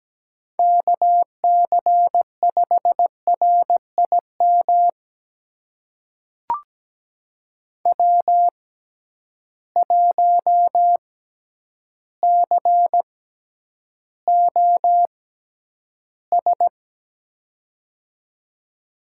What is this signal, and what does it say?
Morse code, heard as a keyed tone, about 17 wpm, spaced as for 9 wpm, 700 Hz. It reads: KC5RIM W1COS
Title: USA Call Signs - 6x Character Spacing - 17wpm